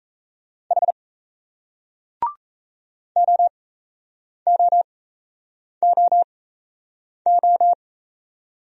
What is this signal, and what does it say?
Morse code, heard as a keyed tone, about 37 wpm, spaced as for 8 wpm, 700 Hz. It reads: H OOOO